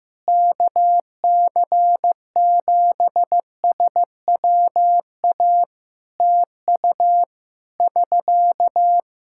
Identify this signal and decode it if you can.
Morse code, heard as a keyed tone, about 15 wpm, 700 Hz. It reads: KC7SWA TU <SK>
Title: Practice QSO - Exchange 4 - Ending - 15wpm